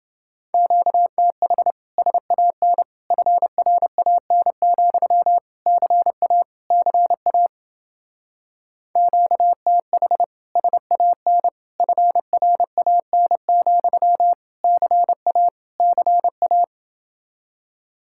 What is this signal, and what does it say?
Morse code, heard as a keyed tone, about 30 wpm, 700 Hz. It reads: QTH SAN FRAN, CA CA QTH SAN FRAN, CA CA